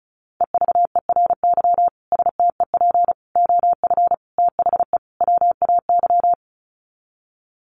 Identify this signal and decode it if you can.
Morse code, heard as a keyed tone, about 34 wpm, 700 Hz. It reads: EVERY STEP OF THE WAY
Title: Sets of 5 Words Encore2 35wpm